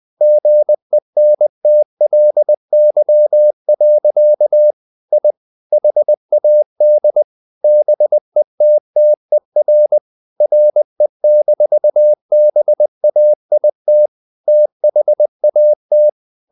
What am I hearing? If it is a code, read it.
Morse code, heard as a keyed tone, about 20 wpm, 600 Hz. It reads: GENTLY. I HAD BETTER RE-BAIT THAT
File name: the old man and the sea / the old man and the sea part 1 20wpm